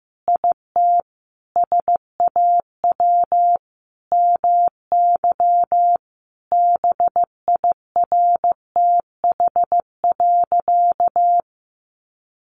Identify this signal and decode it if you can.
Morse code, heard as a keyed tone, about 15 wpm, 700 Hz. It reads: IT SAW MY BIRTH.
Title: Sentences from Top 800 Words 15wpm